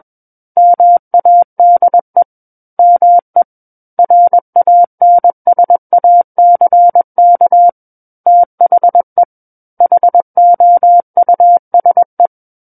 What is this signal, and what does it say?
Morse code, heard as a keyed tone, about 21 wpm, 700 Hz. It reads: MADE ME RANSACK THE HOUSE
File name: Book One / 12-Chapter 12 - 21 WPM